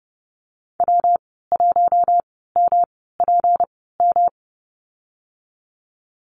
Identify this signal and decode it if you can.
Morse code, heard as a keyed tone, about 30 wpm, spaced as for 17 wpm, 700 Hz. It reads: W1MPM
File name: Lesson 37g - Letter x - Call Signs 30wpm